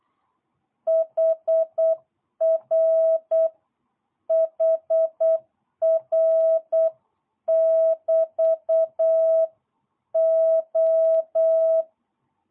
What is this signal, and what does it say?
Morse code, heard as a keyed tone, about 8 wpm, 660 Hz. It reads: HR HR <BT> O
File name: jh8cbh / 2022cw